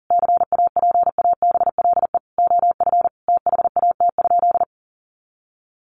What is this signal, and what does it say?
Morse code, heard as a keyed tone, about 40 wpm, 700 Hz. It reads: CAPABLE OF THAT?
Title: Sentences from Top 1200 Words 40wpm